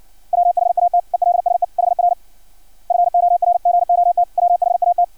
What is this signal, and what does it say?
Morse code, heard as a keyed tone, about 56 wpm, 700 Hz. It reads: WHAT ELSE HR FORGOT WHAT
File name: archived / CRSnet